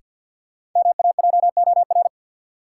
Morse code, heard as a keyed tone, about 51 wpm, 700 Hz. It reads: MAJOR
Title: Sentences from Top 600 Words - New Word Drill - 50wpm